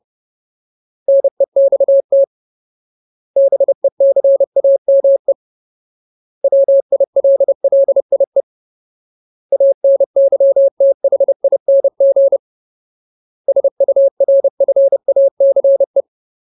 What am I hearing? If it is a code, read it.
Morse code, heard as a keyed tone, about 30 wpm, 550 Hz. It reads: NEXT BECAME WILLIE ANYTHING SURFACE
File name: Ultimate Word List / 30WPM Ultimate